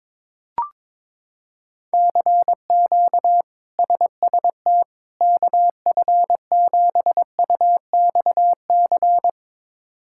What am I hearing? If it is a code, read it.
Morse code, heard as a keyed tone, about 22 wpm, 700 Hz. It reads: CQ SST KF7UXC
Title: K1USN SST QSO Practice - 22wpm